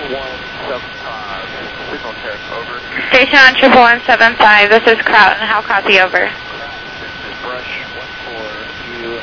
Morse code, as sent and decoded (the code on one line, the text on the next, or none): none